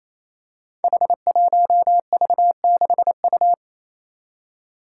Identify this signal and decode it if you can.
Morse code, heard as a keyed tone, about 28 wpm, 700 Hz. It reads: H1V6U
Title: Five Challenging Letter-Number Combinations 28wpm